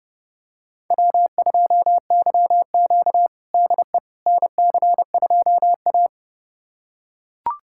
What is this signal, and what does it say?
Morse code, heard as a keyed tone, about 30 wpm, 700 Hz. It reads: W2YQ DE NC2A
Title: USA Callsign DE Callsign - 4 Characters - 30wpm